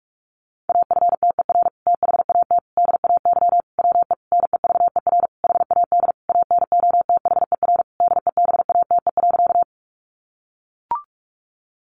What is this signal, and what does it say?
Morse code, heard as a keyed tone, about 44 wpm, 700 Hz. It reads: AFTER THAT DAY WE NEVER HAD ANOTHER DEBATE.
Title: Sentences from Top 1100 Words 45wpm